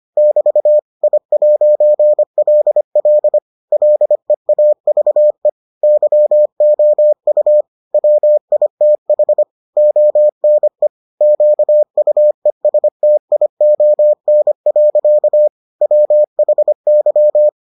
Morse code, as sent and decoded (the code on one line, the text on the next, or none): -...- / .. .----. .-.. .-.. / .-.. .- ...- . / -.-- --- ..- / .-- .. - .... / --- -. / --.- ..- . ... - .. --- -. .-.-.- / .-- .... -.--
<BT> I'LL LEAVE YOU WITH ONE QUESTION. WHY